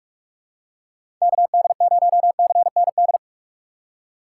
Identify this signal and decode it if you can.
Morse code, heard as a keyed tone, about 46 wpm, 700 Hz. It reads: KD0CND
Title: USA Call Signs - 6 Characters - 4x Character Spacing - 45wpm